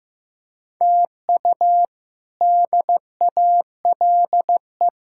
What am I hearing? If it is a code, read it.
Morse code, heard as a keyed tone, about 15 wpm, 700 Hz. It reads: TU DALE
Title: K1USN SST QSO Practice - 15wpm